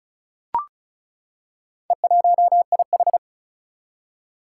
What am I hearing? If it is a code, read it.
Morse code, heard as a keyed tone, about 36 wpm, 700 Hz. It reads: E1IH
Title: Four Challenging Letter-Number Combinations 35wpm